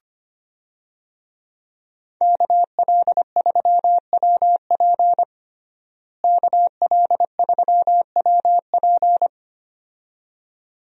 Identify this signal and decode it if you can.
Morse code, heard as a keyed tone, about 25 wpm, 700 Hz. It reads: KL3WP KL3WP
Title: USA Call Signs - 5 Characters - Repeated 2x - 25wpm